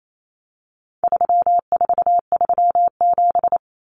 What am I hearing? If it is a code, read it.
Morse code, heard as a keyed tone, about 28 wpm, 700 Hz. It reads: 3437
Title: Common Cut Numbers - 4-Digit 28wpm